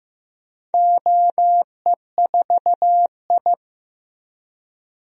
Morse code, heard as a keyed tone, about 15 wpm, 700 Hz. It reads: OE4I
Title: DXCC-100 Call Signs - 4 Characters - Repeated 3x - 15wpm